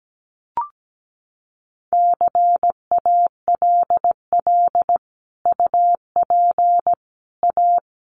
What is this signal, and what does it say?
Morse code, heard as a keyed tone, about 17 wpm, 700 Hz. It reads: CALL UP A
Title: Sentences from Top 1500 Words 17wpm